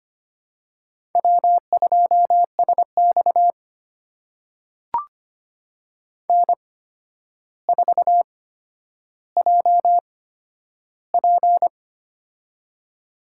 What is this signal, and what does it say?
Morse code, heard as a keyed tone, about 25 wpm, spaced as for 11 wpm, 700 Hz. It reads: W2SX N4JP
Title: USA Call Signs - 4 Characters - 8x Character Spacing - 25wpm